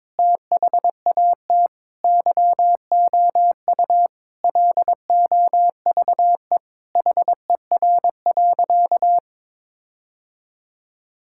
Morse code, heard as a keyed tone, about 22 wpm, 700 Hz. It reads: THAT YOU LOVE HER.
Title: Sentences from Top 1600 Words 22wpm